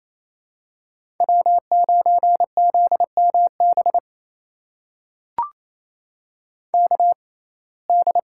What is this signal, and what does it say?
Morse code, heard as a keyed tone, about 28 wpm, spaced as for 20 wpm, 700 Hz. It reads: W9ZMB KD
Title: USA Call Signs - 5 Characters - 6x Character Spacing - 28wpm